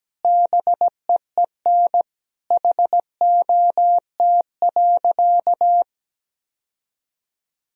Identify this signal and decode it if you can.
Morse code, heard as a keyed tone, about 17 wpm, 700 Hz. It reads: BEEN HOT.
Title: Sentences from Top 1000 Words - Review Part 3 - 17wpm